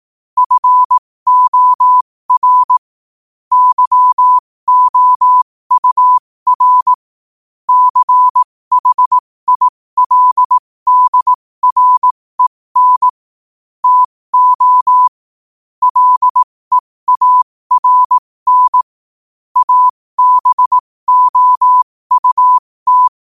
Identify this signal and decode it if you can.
Morse code, heard as a keyed tone, about 18 wpm, spaced as for 15 wpm, 1000 Hz. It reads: FOR YOUR CHILDREN TO LEARN ABOUT
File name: qotdfiles / 15wpm